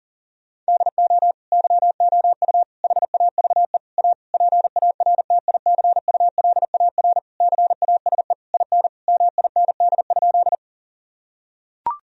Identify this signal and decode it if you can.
Morse code, heard as a keyed tone, about 40 wpm, 700 Hz. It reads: DO YOU HAVE A PARTICULAR CASE IN MIND?
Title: Sentences from Top 1000 Words - Review Part 4 - 40wpm